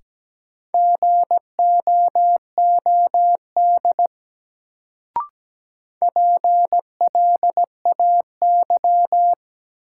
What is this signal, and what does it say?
Morse code, heard as a keyed tone, about 17 wpm, 700 Hz. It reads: GOOD PLAY
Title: Sentences from Top 200 Words - New Word Drill - 17wpm